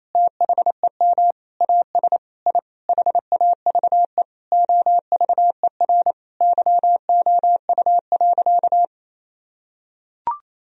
Morse code, heard as a keyed tone, about 28 wpm, 700 Hz. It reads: THEM AS I HAVE OVER YOU.
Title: Sentences from Top 1000 Words - Review Part 1 - 28wpm